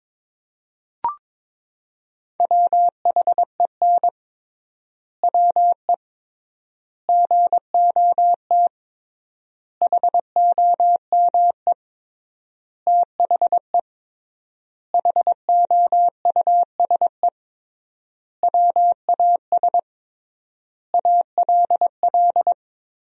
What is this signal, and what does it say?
Morse code, heard as a keyed tone, about 22 wpm, 700 Hz. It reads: WHEN WE GOT HOME THE HOUSE WAS ALL
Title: 3x Word-Spacing - Sentences from Top 500 Words - 22wpm